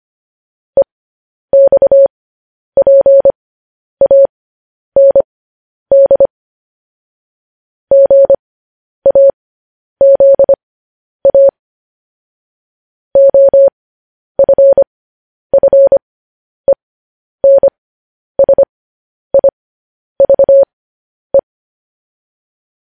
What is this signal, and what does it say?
Morse code, heard as a keyed tone, about 25 wpm, spaced as for 10 wpm, 550 Hz. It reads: EXPAND GAZA OFFENSIVE
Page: news-headlines-in-morse-code-at-10-wpm